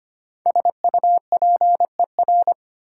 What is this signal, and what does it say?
Morse code, heard as a keyed tone, about 25 wpm, 700 Hz. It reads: SUPER
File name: 5 Letter Words - First 3 Letters Given - 25wpm